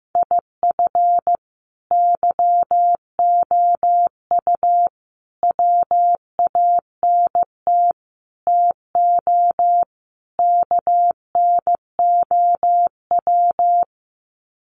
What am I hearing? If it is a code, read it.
Morse code, heard as a keyed tone, about 15 wpm, 700 Hz. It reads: IF YOU WANT TO KNOW
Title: Sets of 5 Words Encore2 15wpm